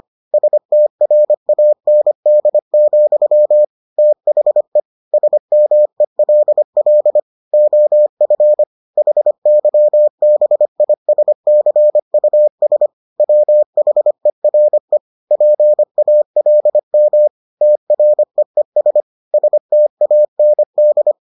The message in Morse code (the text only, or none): STRAND, THE SMELL OF HYBISCUS WHERE PALM TREES STAND